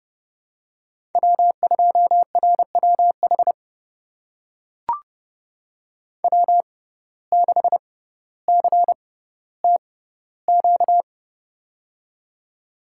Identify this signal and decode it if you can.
Morse code, heard as a keyed tone, about 30 wpm, spaced as for 15 wpm, 700 Hz. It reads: W2RWH W6CTQ
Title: USA Call Signs - 5 Characters - 6x Character Spacing - 30wpm